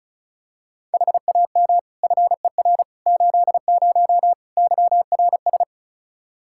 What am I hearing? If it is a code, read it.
Morse code, heard as a keyed tone, about 36 wpm, 700 Hz. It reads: HAM FER 80 YRS
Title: Practice QSO - Exchange 3 - More Info - 35wpm